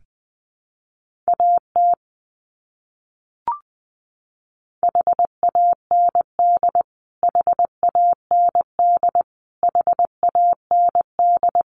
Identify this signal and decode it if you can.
Morse code, heard as a keyed tone, about 20 wpm, 700 Hz. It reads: AT HAND HAND HAND
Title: Top 300 Words Repeated 20wpm